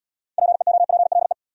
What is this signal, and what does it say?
Morse code, heard as a keyed tone, about 78 wpm, 700 Hz. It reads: PEOPLE